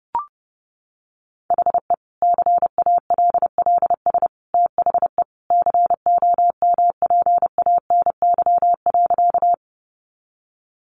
Morse code, heard as a keyed tone, about 30 wpm, 700 Hz. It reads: HE CALLS THE COMPANY.
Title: Additional Easy Phrases 30wpm